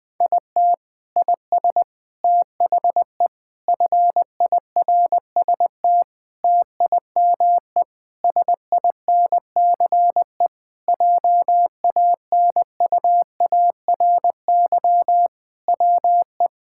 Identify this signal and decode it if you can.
Morse code, heard as a keyed tone, about 20 wpm, 700 Hz. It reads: IT IS THE FIRST TIME SINCE JANUARY WE
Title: Sentences with 4-Word Phrases - 20wpm